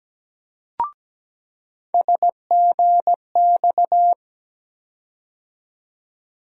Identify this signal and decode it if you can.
Morse code, heard as a keyed tone, about 17 wpm, 700 Hz. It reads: SGX
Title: USA Call Sign Postfixes - 3 Letters 17wpm